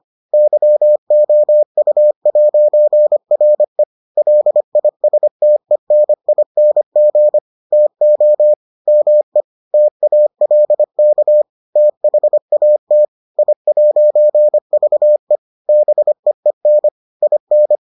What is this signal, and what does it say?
Morse code, heard as a keyed tone, about 25 wpm, 600 Hz. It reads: YOU'RE LISTENING TO ME TALK THAT I'VE BEEN IN